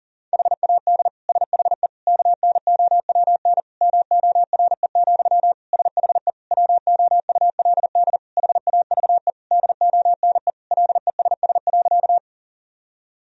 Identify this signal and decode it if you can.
Morse code, heard as a keyed tone, about 40 wpm, 700 Hz. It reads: HAD SHE KNOWN MORE, SHE WOULD HAVE DONE LESS.
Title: Sentences from Top 1000 Words - Review Part 3 - 40wpm